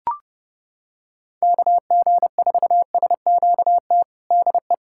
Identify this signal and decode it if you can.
Morse code, heard as a keyed tone, about 30 wpm, 700 Hz. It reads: KG4SQT DE
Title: USA Callsign DE Callsign - 6 Characters - 30wpm